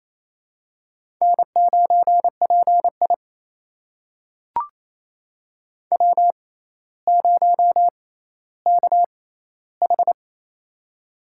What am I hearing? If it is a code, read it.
Morse code, heard as a keyed tone, about 27 wpm, spaced as for 14 wpm, 700 Hz. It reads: N9PI W0KH
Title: USA Call Signs - 4 Characters - 6x Character Spacing - 28wpm